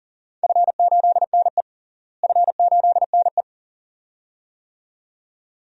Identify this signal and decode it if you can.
Morse code, heard as a keyed tone, about 40 wpm, 700 Hz. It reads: F8NE F8NE
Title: DXCC-100 Call Signs - 4 Characters - Repeated 2x - 40wpm